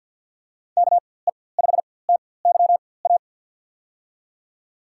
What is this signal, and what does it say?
Morse code, heard as a keyed tone, about 52 wpm, spaced as for 23 wpm, 700 Hz. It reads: KE5TYA
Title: USA Call Signs - 4x Character Spacing - 50wpm